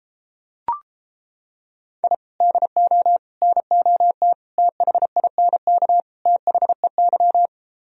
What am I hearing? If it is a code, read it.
Morse code, heard as a keyed tone, about 29 wpm, 700 Hz. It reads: I DO NOT THINK THEY